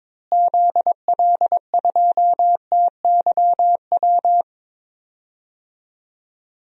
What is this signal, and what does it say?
Morse code, heard as a keyed tone, about 22 wpm, 700 Hz. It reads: ZL2TYW
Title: DXCC-100 Call Signs - 6 Characters - Repeated 2x - 22wpm